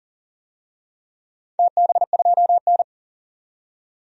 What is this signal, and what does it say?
Morse code, heard as a keyed tone, about 40 wpm, 700 Hz. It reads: TB2N
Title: DXCC-100 Call Signs - 4 Characters - 40wpm